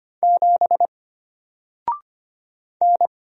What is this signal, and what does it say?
Morse code, heard as a keyed tone, about 25 wpm, 700 Hz. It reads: 7 N
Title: Single Letter-Number Alt 25wpm